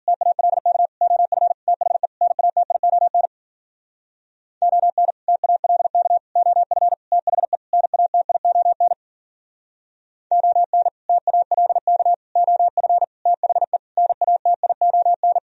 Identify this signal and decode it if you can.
Morse code, heard as a keyed tone, about 49 wpm, 700 Hz. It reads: TALK OF THE NATION ON TALK OF THE NATION ON TALK OF THE NATION
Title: Speed-Racing - Sets of 5 Words 40wpm